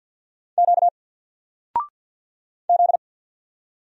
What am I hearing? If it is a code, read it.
Morse code, heard as a keyed tone, about 50 wpm, 700 Hz. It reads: Q Z
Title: Single Letter-Number Alt 50wpm